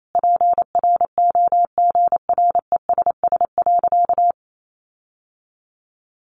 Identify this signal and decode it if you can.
Morse code, heard as a keyed tone, about 28 wpm, 700 Hz. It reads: PROGRESS.
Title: Sentences from Top 1400 Words 28wpm